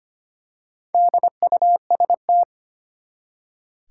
Morse code, heard as a keyed tone, about 25 wpm, 700 Hz. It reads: DUST
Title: Four Letter Words - 25wpm